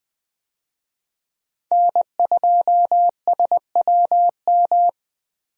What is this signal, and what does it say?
Morse code, heard as a keyed tone, about 20 wpm, 700 Hz. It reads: N2SWM
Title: USA Call Signs - 5 Characters - 20wpm